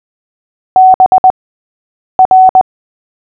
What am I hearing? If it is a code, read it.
Morse code, heard as a keyed tone, about 20 wpm, spaced as for 8 wpm, 750 Hz. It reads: BR
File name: cw / fists / Short-Story-3-8wpm